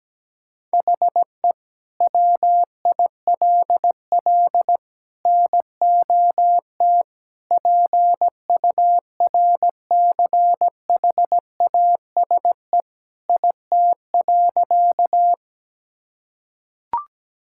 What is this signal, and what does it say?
Morse code, heard as a keyed tone, about 17 wpm, 700 Hz. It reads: HE WILL NOT PURCHASE IT.
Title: Sentences from Top 1000 Words - Review Part 5 - 17wpm